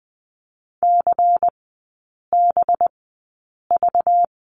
Morse code, heard as a keyed tone, about 19 wpm, spaced as for 8 wpm, 700 Hz. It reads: CBV
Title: 191 Lesson 31e - Letter v - Three Characters 20wpm